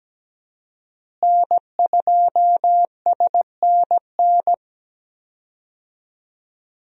Morse code, heard as a keyed tone, about 17 wpm, 700 Hz. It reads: N2SNN
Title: USA Call Signs - 5 Characters - Repeated 3x - 17wpm